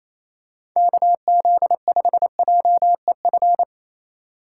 Morse code, heard as a keyed tone, about 28 wpm, 700 Hz. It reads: KZ5JEF
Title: USA Call Signs - 6 Characters - 6x Character Spacing - 28wpm